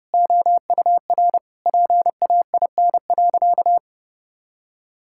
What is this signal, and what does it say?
Morse code, heard as a keyed tone, about 30 wpm, 700 Hz. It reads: OUR PAIN.
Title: Sentences from Top 1600 Words 30wpm